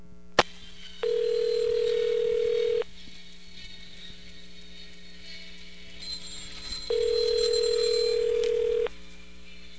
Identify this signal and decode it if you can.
Ringback tone (North America), 440 + 480 Hz.